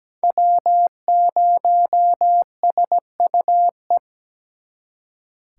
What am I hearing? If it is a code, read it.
Morse code, heard as a keyed tone, about 20 wpm, 700 Hz. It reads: W0SUE
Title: Speed Racing - USA 1x3 Call Signs 17wpm